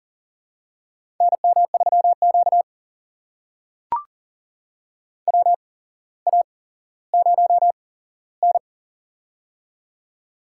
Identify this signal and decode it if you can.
Morse code, heard as a keyed tone, about 40 wpm, spaced as for 17 wpm, 700 Hz. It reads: NM3Q WA0N
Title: USA Call Signs - 4 Characters - 8x Character Spacing - 40wpm